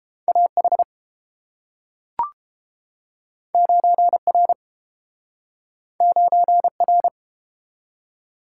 Morse code, heard as a keyed tone, about 32 wpm, 700 Hz. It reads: A H 9 R 9 R